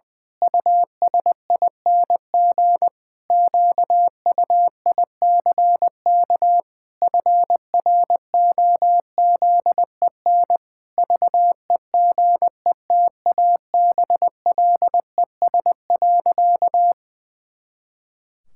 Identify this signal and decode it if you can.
Morse code, heard as a keyed tone, about 20 wpm, 700 Hz. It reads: USING QUICK FROZEN VEGETABLES.
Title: Pangram Challenge v2 - 20wpm